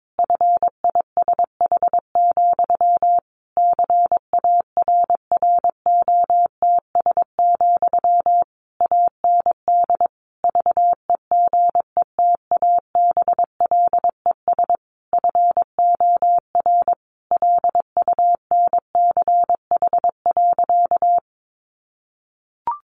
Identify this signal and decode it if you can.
Morse code, heard as a keyed tone, about 22 wpm, 700 Hz. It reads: FISH, CARROTS, AND VEGETABLES FOR LUNCH.
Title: Easy Phrases 22wpm